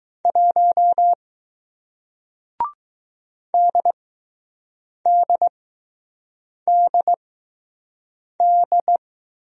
Morse code, heard as a keyed tone, about 20 wpm, spaced as for 6 wpm, 700 Hz. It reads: 1 DDDD